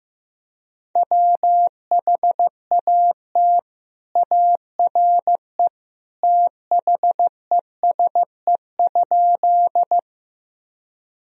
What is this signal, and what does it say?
Morse code, heard as a keyed tone, about 15 wpm, 700 Hz. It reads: WHAT ARE THESE?
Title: Sentences from Top 1000 Words - Review Part 1 - 15wpm